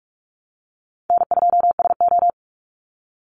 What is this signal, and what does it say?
Morse code, heard as a keyed tone, about 44 wpm, 700 Hz. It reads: N2SO
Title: Call Signs - 45wpm